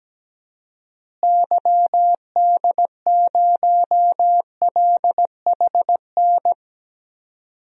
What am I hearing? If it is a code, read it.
Morse code, heard as a keyed tone, about 17 wpm, 700 Hz. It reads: YD0LHN
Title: DXCC-100 Call Signs - 6 Characters - Repeated 2x - 17wpm